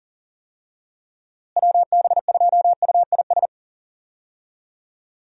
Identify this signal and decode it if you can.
Morse code, heard as a keyed tone, about 37 wpm, 700 Hz. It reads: WB2UIS
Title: USA Call Signs - 6 Characters - Repeated 2x - 40wpm